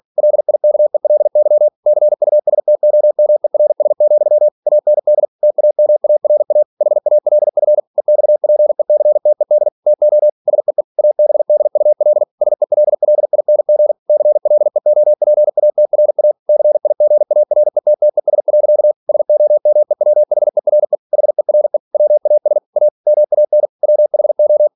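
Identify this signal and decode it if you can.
Morse code, heard as a keyed tone, about 46 wpm, 600 Hz. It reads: LIKELY CUSTOMERS, AND TAMARA HALF EXPECTED TO SEE ABDUL SELLING CLEOPATRA CIGARETTES. SOMEWHERE HERE WAS A MAN WHO
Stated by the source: Practice text from an English novel sent at 47 WPM.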